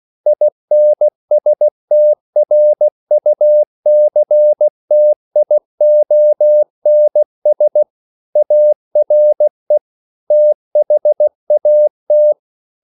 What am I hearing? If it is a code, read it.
Morse code, heard as a keyed tone, about 16 wpm, 600 Hz. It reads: INSTRUCTIONS ARE THAT